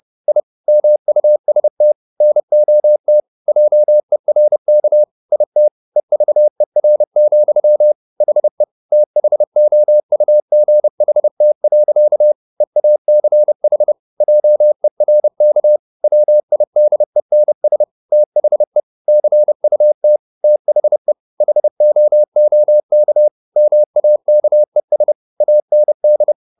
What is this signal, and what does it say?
Morse code, heard as a keyed tone, about 30 wpm, 600 Hz. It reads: I MUST NOT JERK IT EVER, HE THOUGHT. EACH JERK WIDENS THE CUT THE HOOK MAKES AND
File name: the old man and the sea / the old man and the sea 1 30wpm